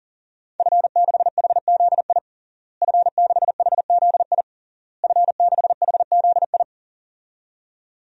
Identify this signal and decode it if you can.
Morse code, heard as a keyed tone, about 40 wpm, 700 Hz. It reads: F6HZI F6HZI F6HZI
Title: DXCC-100 Call Signs - 5 Characters - Repeated 3x - 40wpm